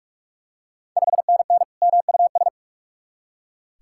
Morse code, heard as a keyed tone, about 45 wpm, 700 Hz. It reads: 5NN MUS
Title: Ontario QSO Party Exchange 45wpm